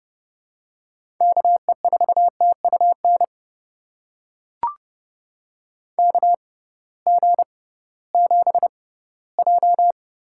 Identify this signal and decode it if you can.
Morse code, heard as a keyed tone, about 30 wpm, spaced as for 17 wpm, 700 Hz. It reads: KE4TUN KG7J
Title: USA Call Signs - 6 Characters - 6x Character Spacing - 30wpm